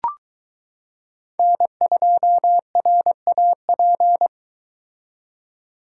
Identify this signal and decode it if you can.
Morse code, heard as a keyed tone, about 22 wpm, 700 Hz. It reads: N2RAP